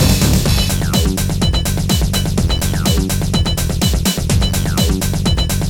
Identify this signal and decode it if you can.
There is no code.